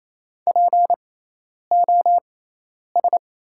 Morse code, heard as a keyed tone, about 28 wpm, spaced as for 10 wpm, 700 Hz. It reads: POS
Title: 6x Character Spacing - Top 500 Words - 28wpm